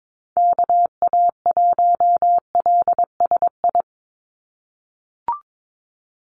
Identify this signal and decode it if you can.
Morse code, heard as a keyed tone, about 22 wpm, 700 Hz. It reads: KA1LSI
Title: USA Call Signs 22wpm